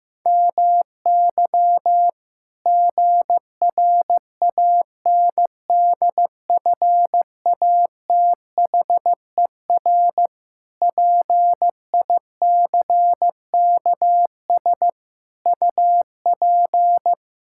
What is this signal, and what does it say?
Morse code, heard as a keyed tone, about 15 wpm, 700 Hz. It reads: MY GRANDFATHER PICKS UP
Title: Pangram Challenge v2 - 15wpm